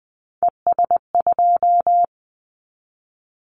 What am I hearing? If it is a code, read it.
Morse code, heard as a keyed tone, about 20 wpm, 700 Hz. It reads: ES2
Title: Callsign Prefixes for Top 50 DXCC - 20wpm